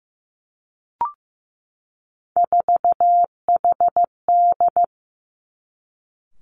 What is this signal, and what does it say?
Morse code, heard as a keyed tone, about 15 wpm, 700 Hz. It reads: 4HD
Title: Three Challenging Letter-Number Combinations 15wpm